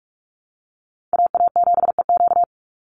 Morse code, heard as a keyed tone, about 44 wpm, 690 Hz. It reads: AA7EQ
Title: NA SOTA Chaser Callsigns 45wpm